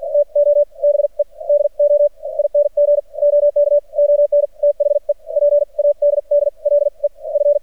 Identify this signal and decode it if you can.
Morse code, heard as a keyed tone, about 50 wpm, spaced as for 38 wpm, 580 Hz. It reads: TO BE DOING OK ON THE PADDLE F
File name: archived / CRSnet